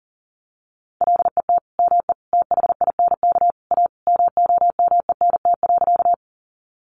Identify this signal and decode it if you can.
Morse code, heard as a keyed tone, about 40 wpm, 700 Hz. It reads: LET ME THINK A MOMENT.
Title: Sentences from Top 1000 Words 40wpm